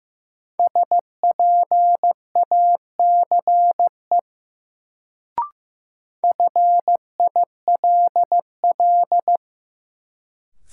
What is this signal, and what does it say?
Morse code, heard as a keyed tone, about 16 wpm, 700 Hz. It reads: SPACE FILL
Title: Sentences from Top 1500 Words - New Word Drill Alt - 15wpm